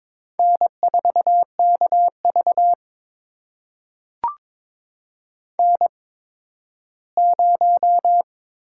Morse code, heard as a keyed tone, about 22 wpm, spaced as for 13 wpm, 700 Hz. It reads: N4KV N0
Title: USA Call Signs - 4 Characters - 8x Character Spacing - 22wpm